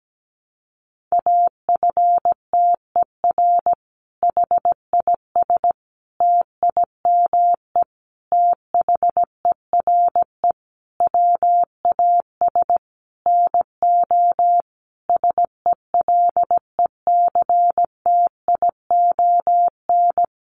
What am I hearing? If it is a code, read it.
Morse code, heard as a keyed tone, about 17 wpm, 700 Hz. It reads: AFTER HIS TIME THERE WAS NO SELECTION
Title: Sentences from Top 1400 Words 17wpm